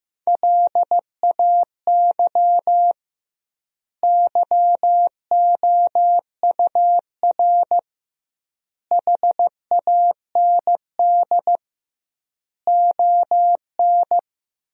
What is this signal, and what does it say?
Morse code, heard as a keyed tone, about 15 wpm, 700 Hz. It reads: LAY YOUR HAND ON
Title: 2x Word-Spacing - Sentences from Top 400 Words - 15wpm